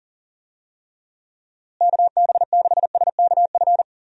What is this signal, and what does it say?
Morse code, heard as a keyed tone, about 40 wpm, 700 Hz. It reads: KB6SKF
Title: USA Call Signs - 6 Characters - Repeated 2x - 40wpm